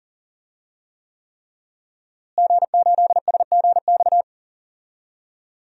Morse code, heard as a keyed tone, about 39 wpm, 700 Hz. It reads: G8SGX
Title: DXCC-100 Call Signs - 5 Characters - Repeated 3x - 40wpm